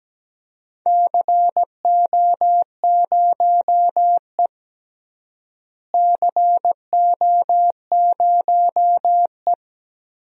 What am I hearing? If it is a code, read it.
Morse code, heard as a keyed tone, about 17 wpm, 700 Hz. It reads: CO0E CO0E
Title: DXCC-100 Call Signs - 4 Characters - Repeated 3x - 17wpm